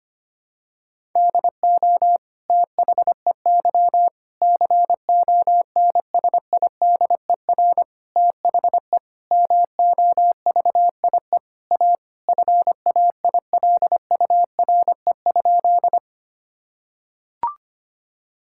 Morse code, heard as a keyed tone, about 25 wpm, 700 Hz. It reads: DO THEY CONSIDER THE MOVIE A FAILURE?
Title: Sentences from Top 1000 Words - Review Part 6 - 25wpm